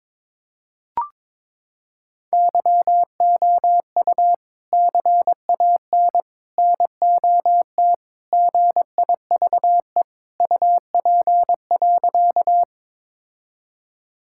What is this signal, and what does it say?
Morse code, heard as a keyed tone, about 22 wpm, 700 Hz. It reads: YOU CAN NOT GIVE UP.
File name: practice / 22wpm / Sentences with 3-Word Phrases - 22wpm